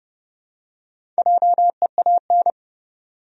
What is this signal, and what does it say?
Morse code, heard as a keyed tone, about 30 wpm, 700 Hz. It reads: JEAN
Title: USA Common Names - Max 5 Letters - 30wpm